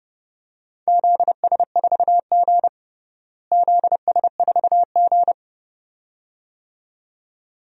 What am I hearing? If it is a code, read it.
Morse code, heard as a keyed tone, about 30 wpm, 700 Hz. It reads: ZS4G ZS4G
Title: DXCC-100 Call Signs - 4 Characters - Repeated 2x - 30wpm